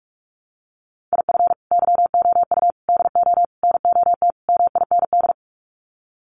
Morse code, heard as a keyed tone, about 44 wpm, 700 Hz. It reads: IF YOU DO NOT MIND
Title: Sets of 5 Words 45wpm